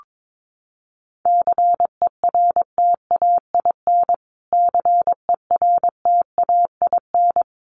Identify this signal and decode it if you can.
Morse code, heard as a keyed tone, about 22 wpm, 700 Hz. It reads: CERTAIN CERTAIN
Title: Top 500 Words Repeated 22wpm